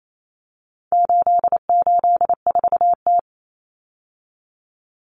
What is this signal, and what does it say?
Morse code, heard as a keyed tone, about 28 wpm, 700 Hz. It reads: 884T
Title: Common Cut Numbers - 4-Digit 28wpm